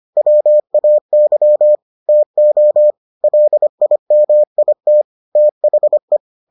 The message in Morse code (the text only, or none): WAY TO LIMIT THE